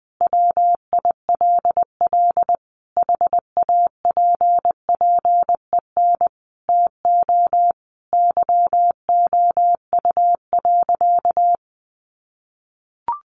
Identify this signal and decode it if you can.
Morse code, heard as a keyed tone, about 20 wpm, 700 Hz. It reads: WILL HAPPEN TO YOU.
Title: Sentences from Top 500 Words 20wpm